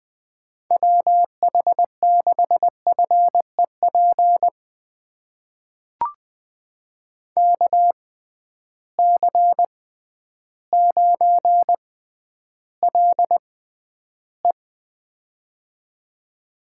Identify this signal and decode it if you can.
Morse code, heard as a keyed tone, about 20 wpm, spaced as for 11 wpm, 700 Hz. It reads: WH6FEP KC9LE
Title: USA Call Signs - 6x Character Spacing - 20wpm